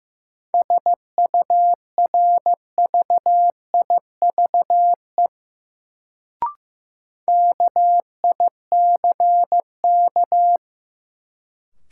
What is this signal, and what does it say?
Morse code, heard as a keyed tone, about 15 wpm, 700 Hz. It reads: SURVIVE KICK
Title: Sentences from Top 1300 Words - New Word Drill - 15wpm